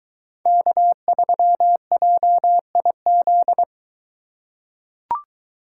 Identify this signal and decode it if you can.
Morse code, heard as a keyed tone, about 22 wpm, 700 Hz. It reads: K3JIZ